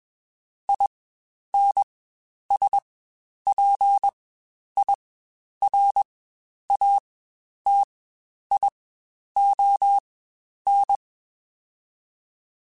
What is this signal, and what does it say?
Morse code, heard as a keyed tone, about 21 wpm, spaced as for 10 wpm, 800 Hz. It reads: INSPIRATION
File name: Quotes Spaced / Quotes Spaced-mp3s / Persperation sp @20